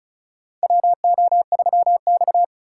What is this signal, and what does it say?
Morse code, heard as a keyed tone, about 35 wpm, 700 Hz. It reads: WO3X
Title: USA Call Signs - 4 Characters - 2x Character Spacing - 35wpm